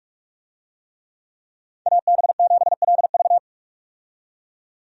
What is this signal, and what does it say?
Morse code, heard as a keyed tone, about 46 wpm, 700 Hz. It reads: AB7LV
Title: USA Call Signs - 5 Characters - 45wpm